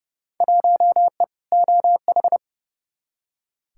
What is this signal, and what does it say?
Morse code, heard as a keyed tone, about 30 wpm, 700 Hz. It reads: 1E OH
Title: Field Day Exchanges - 30wpm